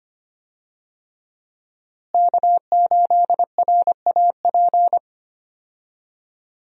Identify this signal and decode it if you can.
Morse code, heard as a keyed tone, about 25 wpm, 700 Hz. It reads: K8RAP
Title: USA Call Signs - 5 Characters - Repeated 3x - 25wpm